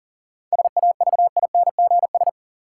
Morse code, heard as a keyed tone, about 40 wpm, 700 Hz. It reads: SAVINGS
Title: Sentences from Top 1200 Words - New Word Drill Alt - 40wpm